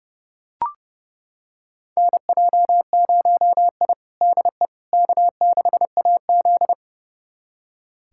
Morse code, heard as a keyed tone, about 30 wpm, 700 Hz. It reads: NJ0I DE K6AZ
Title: USA Callsign DE Callsign - 4 Characters - 30wpm